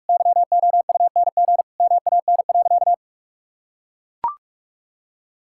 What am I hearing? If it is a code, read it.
Morse code, heard as a keyed tone, about 44 wpm, 700 Hz. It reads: YOUNG MAN.
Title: Sentences from Top 1000 Words - Review Part 5 - 45wpm